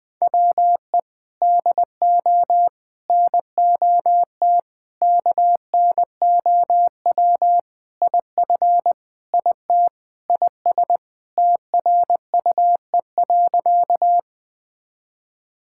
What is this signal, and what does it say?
Morse code, heard as a keyed tone, about 20 wpm, 700 Hz. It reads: WE DO NOT KNOW IF IT IS TRUE.
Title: Sentences with 4-Word Phrases - 20wpm